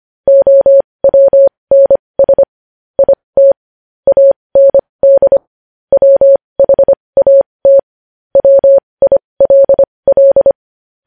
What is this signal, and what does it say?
Morse code, heard as a keyed tone, about 25 wpm, spaced as for 20 wpm, 550 Hz. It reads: OWNS IT AND WHAT WILL
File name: podcast / News headlines-1x-25x20